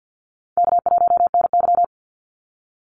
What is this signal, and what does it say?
Morse code, heard as a keyed tone, about 50 wpm, 700 Hz. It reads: K1NY
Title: Easy Call Signs - 50wpm